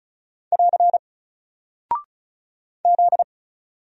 Morse code, heard as a keyed tone, about 36 wpm, 700 Hz. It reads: <AR> Z
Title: Single Character (Letter:Number:Punctuation:Prosign) 35wpm